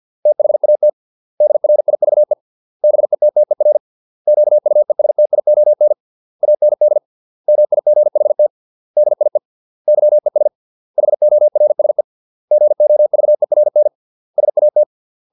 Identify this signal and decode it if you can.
Morse code, heard as a keyed tone, about 50 wpm, 600 Hz. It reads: THAT DRIVE BETTER QUESTION AND MIGHT DIE YES HORSE GOVERN SAT